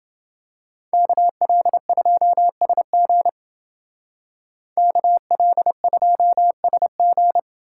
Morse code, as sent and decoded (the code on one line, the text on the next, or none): -.- .-.. ..--- ... --. / -.- .-.. ..--- ... --.
KL2SG KL2SG